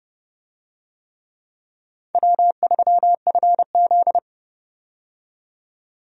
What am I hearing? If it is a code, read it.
Morse code, heard as a keyed tone, about 31 wpm, 700 Hz. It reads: W3FZ
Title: USA Call Signs - 4 Characters - 30wpm